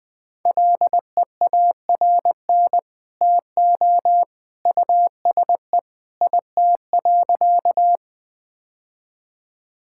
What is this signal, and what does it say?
Morse code, heard as a keyed tone, about 20 wpm, 700 Hz. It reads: LEARN TO USE IT.
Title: Sentences with 2-Word Phrases - 20wpm